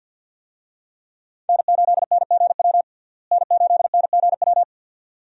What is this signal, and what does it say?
Morse code, heard as a keyed tone, about 49 wpm, 700 Hz. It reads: N8NGW N8NGW
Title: USA Call Signs - 5 Characters - Repeated 2x - 50wpm